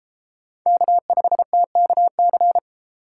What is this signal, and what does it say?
Morse code, heard as a keyed tone, about 32 wpm, 700 Hz. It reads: K5TKC